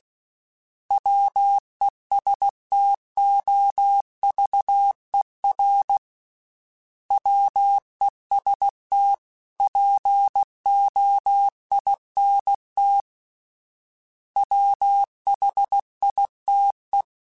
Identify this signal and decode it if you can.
Morse code, heard as a keyed tone, about 16 wpm, 800 Hz. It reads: WESTOVER WEST POINT WHITE
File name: fromSMs / FromSMs mp3s / Cities@15